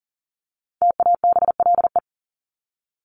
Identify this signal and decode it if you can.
Morse code, heard as a keyed tone, about 40 wpm, 700 Hz. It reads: TABLE
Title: Top 500 Words 40wpm